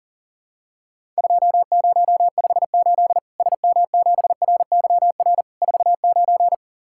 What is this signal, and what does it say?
Morse code, heard as a keyed tone, about 43 wpm, 700 Hz. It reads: 2058 SM7RYR 49
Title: Worked All Europe DX Contest - QTC Exchange - 40wpm